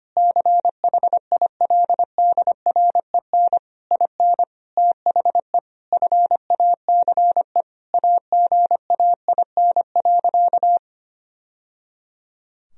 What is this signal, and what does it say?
Morse code, heard as a keyed tone, about 25 wpm, 700 Hz. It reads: CHILDREN IN THE FACE AGAIN.
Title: Sentences from Top 1000 Words - Review Part 2 - 25wpm